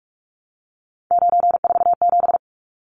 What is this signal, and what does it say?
Morse code, heard as a keyed tone, about 44 wpm, 700 Hz. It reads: 947
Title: Numbers - 3-Digit 45wpm